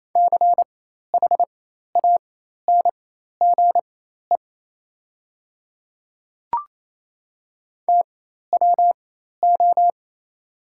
Morse code, heard as a keyed tone, about 28 wpm, spaced as for 13 wpm, 700 Hz. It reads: CHANGE TWO
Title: 4x Character Spacing - Top 300 Words - 28wpm